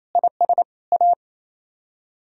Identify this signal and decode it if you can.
Morse code, heard as a keyed tone, about 34 wpm, 700 Hz. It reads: IS A
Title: Speed-Racing - Sets of 5 Words 28wpm